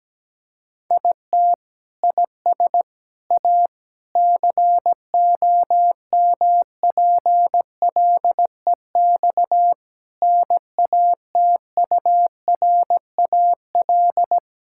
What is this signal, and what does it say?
Morse code, heard as a keyed tone, about 17 wpm, 700 Hz. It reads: IT IS A COMPLEX NATURAL
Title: Sentences from Top 1000 Words - Review Part 6 - 17wpm